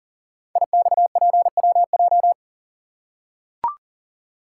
Morse code, heard as a keyed tone, about 39 wpm, 700 Hz. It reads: IXPWJ
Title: Code Groups - 5 Letters - 40wpm